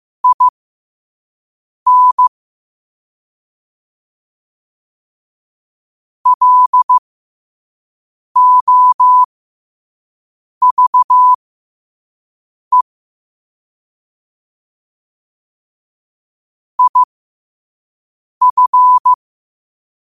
Morse code, heard as a keyed tone, about 15 wpm, spaced as for 5 wpm, 1000 Hz. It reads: IN LOVE IF
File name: qotdfiles / 5wpm